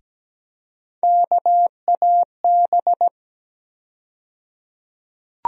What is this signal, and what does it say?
Morse code, heard as a keyed tone, about 17 wpm, 700 Hz. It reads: KAB
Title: USA Call Sign Postfixes - 3 Letters 17wpm